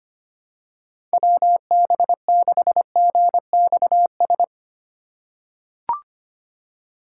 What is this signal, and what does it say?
Morse code, heard as a keyed tone, about 25 wpm, 700 Hz. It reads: WB6GXS
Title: USA Call Signs - 6 Characters - 2x Character Spacing - 25wpm